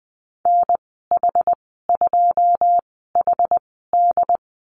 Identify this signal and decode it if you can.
Morse code, heard as a keyed tone, about 20 wpm, 700 Hz. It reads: N H 2 H D
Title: 082 Lesson 14g - Number 2 - Call Signs 20wpm